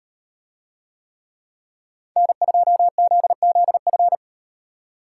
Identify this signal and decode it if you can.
Morse code, heard as a keyed tone, about 38 wpm, 700 Hz. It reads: N2ZZF